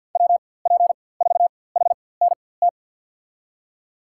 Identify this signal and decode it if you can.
Morse code, heard as a keyed tone, about 53 wpm, spaced as for 24 wpm, 700 Hz. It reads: WP4HNT
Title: USA Call Signs - 6 Characters - 4x Character Spacing - 50wpm